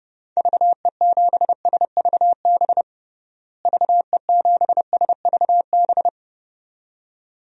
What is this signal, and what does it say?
Morse code, heard as a keyed tone, about 30 wpm, 700 Hz. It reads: VE7SVB VE7SVB
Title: DXCC-100 Call Signs - 6 Characters - Repeated 2x - 30wpm